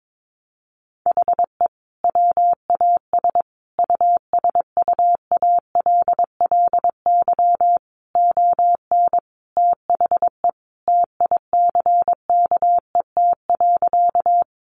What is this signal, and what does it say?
Morse code, heard as a keyed tone, about 22 wpm, 700 Hz. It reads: HE WAS USUALLY ON THE TICKET.
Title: Sentences from Top 1400 Words 22wpm